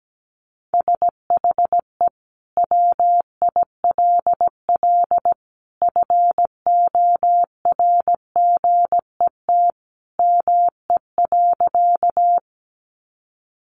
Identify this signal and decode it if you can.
Morse code, heard as a keyed tone, about 17 wpm, 700 Hz. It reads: SHE WILL FORGET ME.
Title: Sentences from Top 1200 Words 17wpm